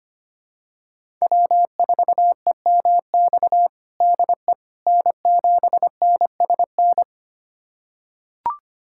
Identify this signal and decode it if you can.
Morse code, heard as a keyed tone, about 25 wpm, 700 Hz. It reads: W4EMX DE N7NSN
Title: USA Callsign DE Callsign - 5 Characters - 25wpm